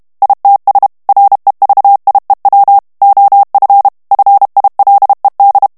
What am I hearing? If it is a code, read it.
Morse code, heard as a keyed tone, about 32 wpm, 800 Hz. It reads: ITS REVIEW OF FILED
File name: ANGLAIS texte clair 3 vitesse 30 mots minute